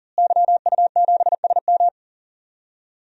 Morse code, heard as a keyed tone, about 38 wpm, 700 Hz. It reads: YU7SM
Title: DXCC-100 Call Signs - 5 Characters - Repeated 3x - 40wpm